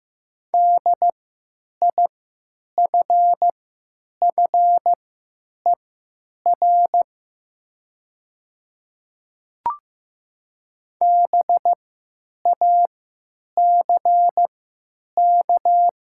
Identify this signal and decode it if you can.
Morse code, heard as a keyed tone, about 15 wpm, spaced as for 9 wpm, 700 Hz. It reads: DIFFER BACK
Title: 3x Character Spacing - Top 200 Words - 15wpm